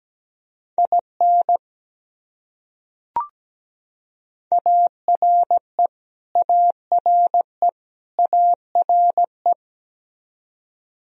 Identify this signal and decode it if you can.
Morse code, heard as a keyed tone, about 17 wpm, 700 Hz. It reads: IN ARE ARE ARE
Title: Top 100 Words Repeated 17wpm